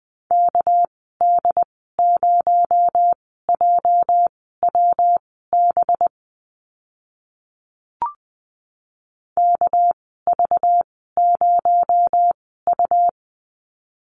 Morse code, heard as a keyed tone, about 20 wpm, spaced as for 14 wpm, 700 Hz. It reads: KD0JWB KV0U
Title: 245 Lesson 39g - Letter z - Call Signs 20wpm